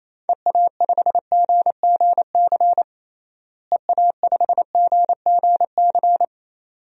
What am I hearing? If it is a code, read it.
Morse code, heard as a keyed tone, about 27 wpm, 700 Hz. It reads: EA5GGC EA5GGC
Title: DXCC-100 Call Signs - 6 Characters - Repeated 2x - 28wpm